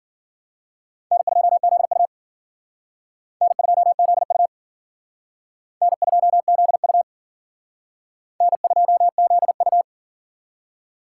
Morse code, heard as a keyed tone, about 54 wpm, 700 Hz. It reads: N2ZU N2ZU N2ZU N2ZU